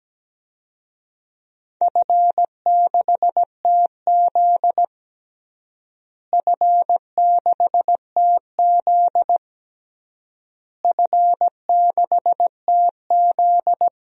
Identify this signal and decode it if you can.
Morse code, heard as a keyed tone, about 17 wpm, 700 Hz. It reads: F6TZ F6TZ F6TZ
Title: DXCC-100 Call Signs - 4 Characters - Repeated 3x - 17wpm